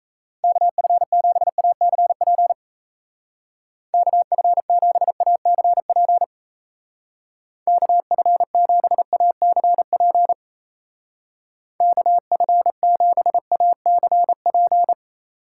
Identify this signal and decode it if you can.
Morse code, heard as a keyed tone, about 34 wpm, 700 Hz. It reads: KF7ACP KF7ACP KF7ACP KF7ACP
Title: Speed Racing - USA 2x3 Call Signs 28wpm